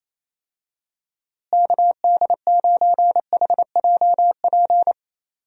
Morse code, heard as a keyed tone, about 28 wpm, 700 Hz. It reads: KD9HJP
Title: USA Call Signs - 6 Characters - Repeated 2x - 28wpm